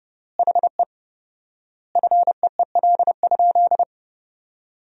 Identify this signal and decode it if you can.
Morse code, heard as a keyed tone, about 32 wpm, 700 Hz. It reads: HE FEEL?
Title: 4x Word-Spacing - Sentences from Top 400 Words - 30wpm